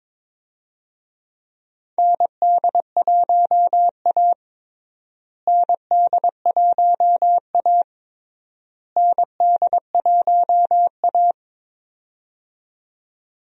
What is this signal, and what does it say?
Morse code, heard as a keyed tone, about 22 wpm, 700 Hz. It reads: ND1A ND1A ND1A
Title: USA Call Signs - 4 Characters - Repeated 3x - 22wpm